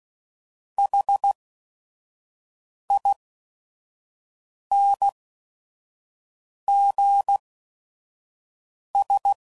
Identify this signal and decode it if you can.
Morse code, heard as a keyed tone, about 16 wpm, spaced as for 5 wpm, 800 Hz. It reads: HINGS
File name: CW Code Groups @5WPM / Technical Subjects / AND Gate